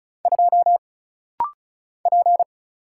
Morse code, heard as a keyed tone, about 36 wpm, 700 Hz. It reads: 2 P
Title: Single Letter-Number - Rapid-Fire 35wpm